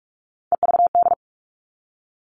Morse code, heard as a keyed tone, about 44 wpm, 700 Hz. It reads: EVD
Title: All Cut Numbers - 3-Digit 45wpm